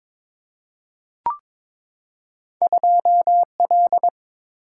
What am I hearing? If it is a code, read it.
Morse code, heard as a keyed tone, about 22 wpm, 700 Hz. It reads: 2L
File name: practice / 22wpm / Two Letter-Number Combinations - 22wpm